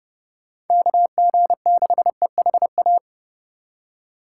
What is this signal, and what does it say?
Morse code, heard as a keyed tone, about 30 wpm, 700 Hz. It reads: KG6EHA
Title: USA Call Signs - 6 Characters - 3x Character Spacing - 30wpm